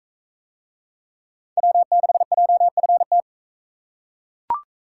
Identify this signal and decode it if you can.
Morse code, heard as a keyed tone, about 40 wpm, 700 Hz. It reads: W6JFT